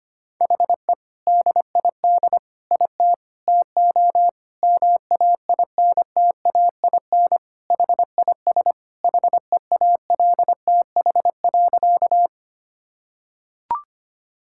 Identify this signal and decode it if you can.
Morse code, heard as a keyed tone, about 25 wpm, 700 Hz. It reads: HE DID IT TO MAINTAIN HIS HEALTH.
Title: Sentences from Top 1000 Words - Review Part 5 - 25wpm